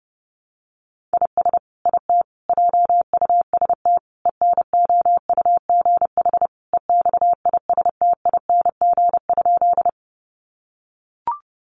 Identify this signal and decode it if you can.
Morse code, heard as a keyed tone, about 28 wpm, 700 Hz. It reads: IS IT JUST ENOUGH EXISTING?
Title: Sentences from Top 1500 Words 30wpm